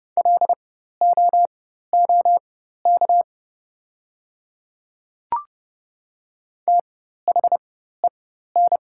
Morse code, heard as a keyed tone, about 31 wpm, spaced as for 14 wpm, 700 Hz. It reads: LOOK THEN
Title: 4x Character Spacing - Top 100 Words - 30wpm